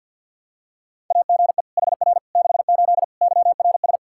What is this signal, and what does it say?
Morse code, heard as a keyed tone, about 52 wpm, 700 Hz. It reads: AGE HR 68 YRS
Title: Practice QSO - Exchange 3 - More Info - 50wpm